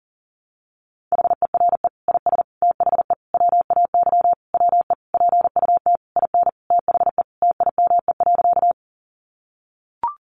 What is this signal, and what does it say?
Morse code, heard as a keyed tone, about 40 wpm, 700 Hz. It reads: HERE IS THE WAY WE PUT IN THE TIME.
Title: Sentences from Top 200 Words 40wpm